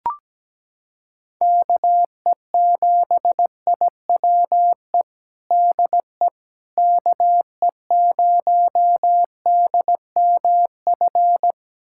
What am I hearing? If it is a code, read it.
Morse code, heard as a keyed tone, about 17 wpm, 700 Hz. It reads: KE7IWE DE KE0DMF
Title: USA Callsign DE Callsign 17wpm